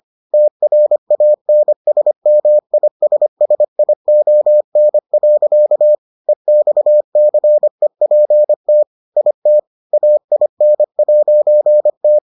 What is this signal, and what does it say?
Morse code, heard as a keyed tone, about 25 wpm, 600 Hz. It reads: TRANSMISSION. EXCEPT IT AIN'T